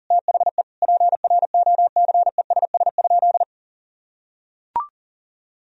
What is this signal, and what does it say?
Morse code, heard as a keyed tone, about 40 wpm, 700 Hz. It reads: THE PROCESS?
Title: Sentences from Top 1000 Words - Review Part 3 - 40wpm